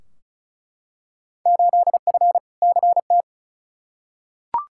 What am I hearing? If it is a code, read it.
Morse code, heard as a keyed tone, about 36 wpm, 700 Hz. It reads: 8F CT
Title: Field Day Exchanges Encore - 35wpm